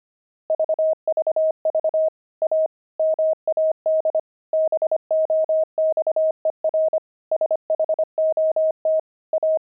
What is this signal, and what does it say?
Morse code, heard as a keyed tone, about 25 wpm, 630 Hz. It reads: VVV A MAD BOXER SHOT A
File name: pangram-fs-25-cs-25-gloved